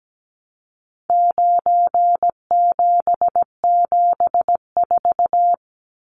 Morse code, heard as a keyed tone, about 17 wpm, 700 Hz. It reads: 9774
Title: Numbers - 4-Digit 17wpm